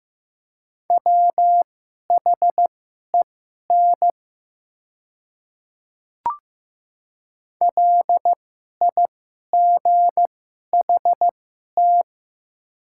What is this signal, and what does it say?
Morse code, heard as a keyed tone, about 15 wpm, spaced as for 11 wpm, 690 Hz. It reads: WHEN LIGHT
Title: 2x Character Spacing - Top 300 Words - 15wpm